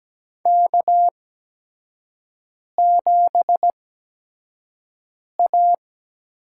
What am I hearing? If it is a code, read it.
Morse code, heard as a keyed tone, about 17 wpm, spaced as for 5 wpm, 700 Hz. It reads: K7A
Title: USA Call Signs - 5 Characters - 8x Character Spacing - 17wpm